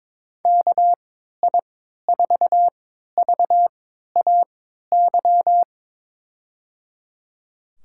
Morse code, heard as a keyed tone, about 23 wpm, spaced as for 13 wpm, 700 Hz. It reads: KI4VAY
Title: USA Call Signs - 3x Character Spacing - 22wpm